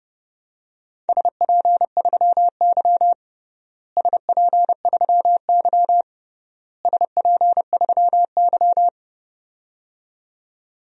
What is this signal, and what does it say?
Morse code, heard as a keyed tone, about 30 wpm, 700 Hz. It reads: SP3Y SP3Y SP3Y
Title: DXCC-100 Call Signs - 4 Characters - Repeated 3x - 30wpm